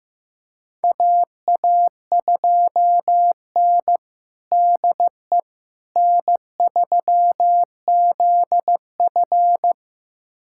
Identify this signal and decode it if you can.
Morse code, heard as a keyed tone, about 15 wpm, 700 Hz. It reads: AA2N DE N3ZF
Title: USA Callsign DE Callsign - 4 Characters - 15wpm